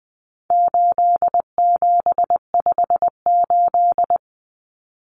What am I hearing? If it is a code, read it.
Morse code, heard as a keyed tone, about 20 wpm, 700 Hz. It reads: 8758
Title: Numbers - 4-Digit 20wpm